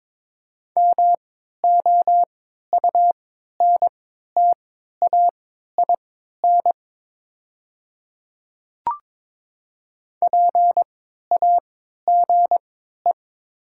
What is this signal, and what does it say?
Morse code, heard as a keyed tone, about 23 wpm, spaced as for 13 wpm, 700 Hz. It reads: MOUNTAIN PAGE
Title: 3x Character Spacing - Top 500 Words - 22wpm